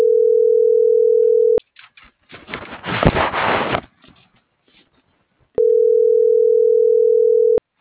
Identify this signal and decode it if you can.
Ringback tone (North America), 440 + 480 Hz.